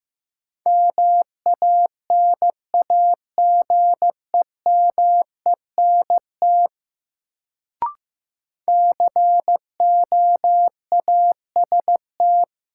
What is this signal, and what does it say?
Morse code, heard as a keyed tone, about 15 wpm, 700 Hz. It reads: MANAGEMENT COAST
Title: Sentences from Top 1000 Words - Review Part 4 - New Word Drill - 15wpm